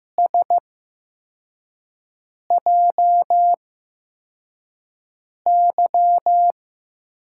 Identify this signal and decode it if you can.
Morse code, heard as a keyed tone, about 15 wpm, spaced as for 4 wpm, 700 Hz. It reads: SJY
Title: USA Call Signs - 6 Characters - 8x Character Spacing - 15wpm